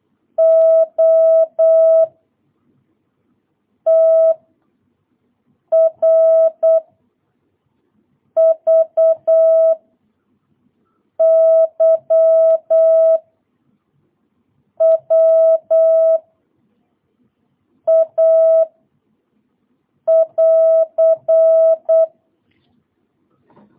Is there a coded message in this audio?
Yes: Morse code from 0.4 to 22.1 s, heard as a keyed tone, about 8 wpm, 660 Hz. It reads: O T R V Y W A <AR>